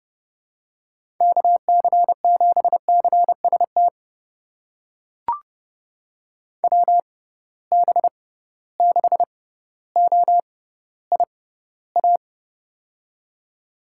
Morse code, heard as a keyed tone, about 32 wpm, spaced as for 16 wpm, 700 Hz. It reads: KC7CST WB6OIA
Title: USA Call Signs - 6 Characters - 6x Character Spacing - 30wpm